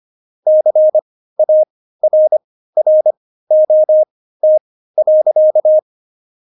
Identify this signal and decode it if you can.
Morse code, heard as a keyed tone, about 25 wpm, spaced as for 15 wpm, 620 Hz. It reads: CARROT.
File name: Jokes-25-char-15-eff